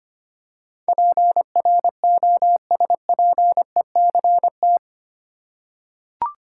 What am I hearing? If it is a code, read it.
Morse code, heard as a keyed tone, about 25 wpm, 700 Hz. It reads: PROSPECT
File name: 8 Letter Words - First 3 Letters Given - 25wpm